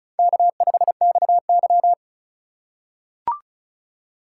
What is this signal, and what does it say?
Morse code, heard as a keyed tone, about 34 wpm, 700 Hz. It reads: K5XY
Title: USA Call Signs - 4 Characters - 2x Character Spacing - 35wpm